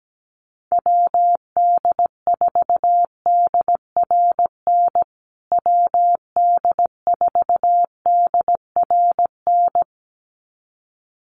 Morse code, heard as a keyed tone, about 17 wpm, 700 Hz. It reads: WD4DRN WD4DRN
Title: USA Call Signs Repeated 2x 17wpm